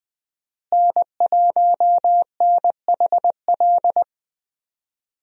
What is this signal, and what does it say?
Morse code, heard as a keyed tone, about 20 wpm, 700 Hz. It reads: N1NHL
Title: USA Call Signs - 5 Characters - Repeated 2x - 20wpm